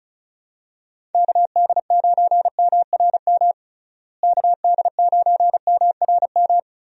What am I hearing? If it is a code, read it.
Morse code, heard as a keyed tone, about 34 wpm, 700 Hz. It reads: KD9MRM KD9MRM
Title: USA Call Signs - 6 Characters - Repeated 2x - 35wpm